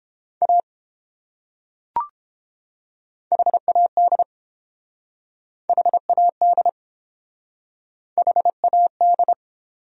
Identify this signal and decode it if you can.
Morse code, heard as a keyed tone, about 31 wpm, 700 Hz. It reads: A HAD HAD HAD